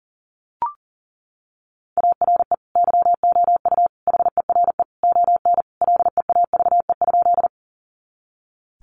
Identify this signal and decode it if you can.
Morse code, heard as a keyed tone, about 40 wpm, 700 Hz. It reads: ARE YOU HERE ON LEAVE?
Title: Sentences from Top 400 Words 40wpm